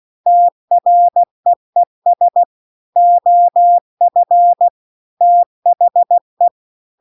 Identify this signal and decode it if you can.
Morse code, heard as a keyed tone, about 16 wpm, 700 Hz. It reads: TREES OF THE
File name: CW-The-Big-Twoheaded-River-2-16WPM